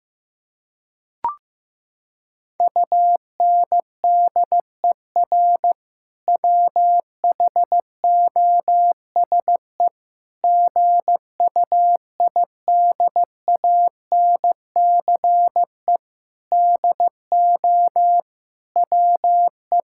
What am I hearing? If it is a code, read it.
Morse code, heard as a keyed tone, about 16 wpm, 700 Hz. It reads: UNDER WHOSE GUIDANCE DO WE
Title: Sentences from Top 1600 Words 15wpm